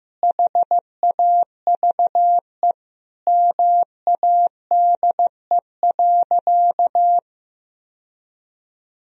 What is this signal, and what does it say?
Morse code, heard as a keyed tone, about 15 wpm, 700 Hz. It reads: HAVE MADE.
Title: Sentences from Top 1000 Words - Review Part 2 - 15wpm